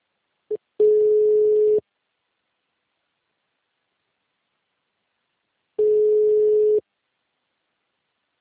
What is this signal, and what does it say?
Ringback tone (Europe, 425 Hz).
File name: out- -2008- - - .6